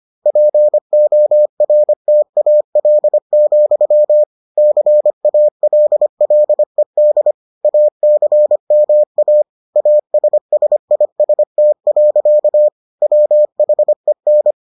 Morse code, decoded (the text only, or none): PORTAL, CALLED ACMA ASSIST. WHEN